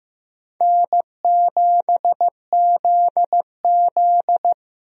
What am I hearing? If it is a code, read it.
Morse code, heard as a keyed tone, about 15 wpm, 700 Hz. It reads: N7ZZ
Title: USA Call Signs - 4 Characters - 15wpm